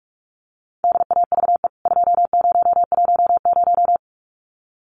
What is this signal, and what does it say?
Morse code, heard as a keyed tone, about 44 wpm, 700 Hz. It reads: DAVE 2010
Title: CWT Exchange 45wpm